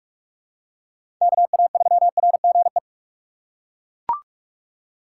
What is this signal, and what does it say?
Morse code, heard as a keyed tone, about 41 wpm, 700 Hz. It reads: KA3RGE
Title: USA Call Signs - 4x Character Spacing - 45wpm